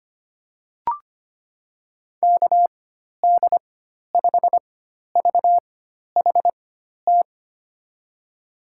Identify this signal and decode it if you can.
Morse code, heard as a keyed tone, about 25 wpm, spaced as for 12 wpm, 700 Hz. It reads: KD5VHT
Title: USA Call Signs - 6 Characters - 4x Character Spacing - 25wpm